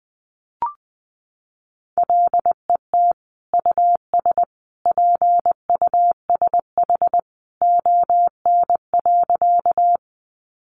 Morse code, heard as a keyed tone, about 20 wpm, 700 Hz. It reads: LET US PUSH ON.
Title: Sentences from Top 1300 Words 20wpm